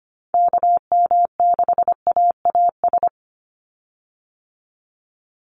Morse code, heard as a keyed tone, about 25 wpm, 700 Hz. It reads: KM6AAS
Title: USA Call Signs 25wpm